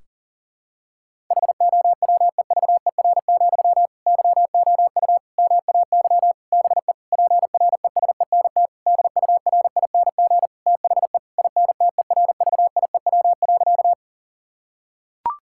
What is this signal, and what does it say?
Morse code, heard as a keyed tone, about 40 wpm, 700 Hz. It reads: HOWEVER, YOU MAY BE PRESENT DURING THE INTERVIEW.
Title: Sentences from Top 1600 Words 40wpm